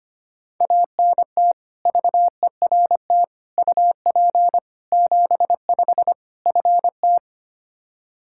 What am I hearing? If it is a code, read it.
Morse code, heard as a keyed tone, about 25 wpm, 700 Hz. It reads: ANT VERT UP 75 FT
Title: K1USN SST QSO Practice 25wpm